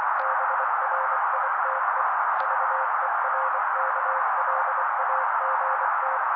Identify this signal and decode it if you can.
Morse code, heard as a keyed tone, about 24 wpm, 530 Hz. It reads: BRITE VERKLAGT
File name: daily / 23wpm noise